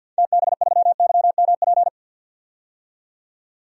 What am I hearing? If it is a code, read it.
Morse code, heard as a keyed tone, about 52 wpm, 700 Hz. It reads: TB3YMP
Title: DXCC-100 Call Signs - 6 Characters - 50wpm